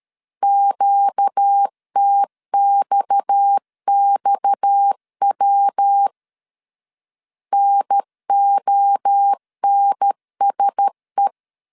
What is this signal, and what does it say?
Morse code, heard as a keyed tone, about 13 wpm, 800 Hz. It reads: QTXXW NONSE
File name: CW / DET / 1 6 1500C S B 12wpm 3431